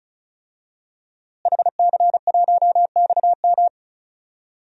Morse code, heard as a keyed tone, about 36 wpm, 700 Hz. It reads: HC1XM
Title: DXCC-100 Call Signs - 5 Characters - Repeated 3x - 35wpm